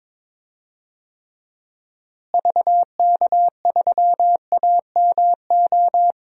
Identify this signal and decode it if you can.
Morse code, heard as a keyed tone, about 22 wpm, 700 Hz. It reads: VK3AMO
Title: DXCC-100 Call Signs - 6 Characters - Repeated 3x - 22wpm